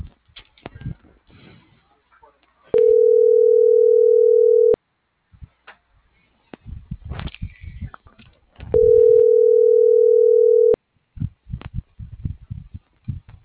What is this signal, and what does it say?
Ringback tone (North America), 440 + 480 Hz.